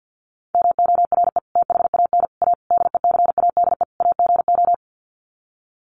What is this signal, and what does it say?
Morse code, heard as a keyed tone, about 50 wpm, 700 Hz. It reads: MORE THAN A DECADE AGO
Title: Sets of 5 Words Encore2 50wpm